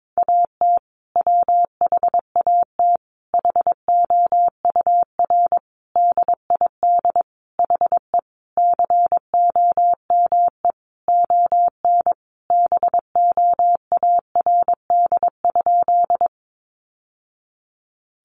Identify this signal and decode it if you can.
Morse code, heard as a keyed tone, about 22 wpm, 700 Hz. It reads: AT WHAT HOUR DID HE COME ON BOARD?
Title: Sentences from Top 700 Words 22wpm